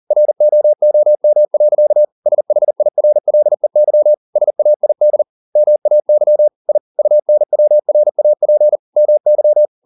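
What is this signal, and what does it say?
Morse code, heard as a keyed tone, about 40 wpm, 600 Hz. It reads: ROOM. SHIRLEY SAID MAY I UNWRAP MY
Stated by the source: Morse code at 40 wpm, practice text, CW med Gnister.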